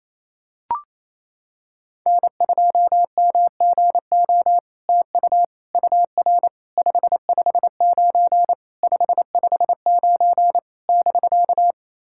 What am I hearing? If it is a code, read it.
Morse code, heard as a keyed tone, about 28 wpm, 700 Hz. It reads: N2MGO TU UR 559 559 <BK>
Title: SOTA QSO Practice 28wpm